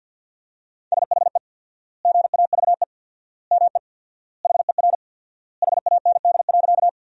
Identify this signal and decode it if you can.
Morse code, heard as a keyed tone, about 50 wpm, 700 Hz. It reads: SHE GAVE ME HER HAND.
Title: 4x Word-Spacing - Sentences from Top 500 Words - 50wpm